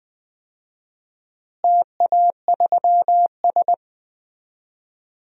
Morse code, heard as a keyed tone, about 20 wpm, 700 Hz. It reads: TA3S
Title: DXCC-100 Call Signs - 4 Characters - Repeated 3x - 20wpm